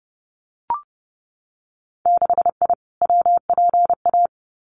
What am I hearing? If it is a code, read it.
Morse code, heard as a keyed tone, about 30 wpm, 700 Hz. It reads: 6I WPA
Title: Winter Field Day Exchanges Encore 30wpm